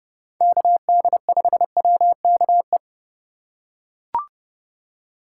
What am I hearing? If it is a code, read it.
Morse code, heard as a keyed tone, about 30 wpm, 700 Hz. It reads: KD5WKE
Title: USA Call Signs - 6 Characters - 3x Character Spacing - 30wpm